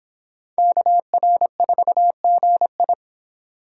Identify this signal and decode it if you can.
Morse code, heard as a keyed tone, about 25 wpm, 700 Hz. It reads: KR4GI